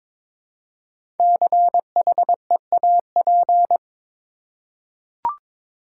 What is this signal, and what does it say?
Morse code, heard as a keyed tone, about 22 wpm, 700 Hz. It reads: CHEAP
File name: 5 Letter Words - First 3 Letters Given - 22wpm